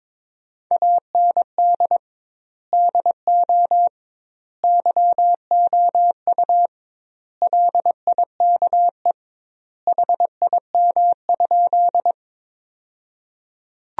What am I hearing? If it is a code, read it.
Morse code, heard as a keyed tone, about 22 wpm, 700 Hz. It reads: AND DO YOU LIKE HIM?
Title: 2x Word-Spacing - Sentences from Top 100 Words - 22wpm